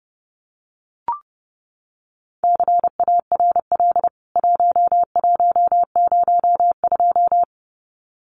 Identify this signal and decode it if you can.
Morse code, heard as a keyed tone, about 30 wpm, 700 Hz. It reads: CARL 1102
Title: CWT Exchange 30wpm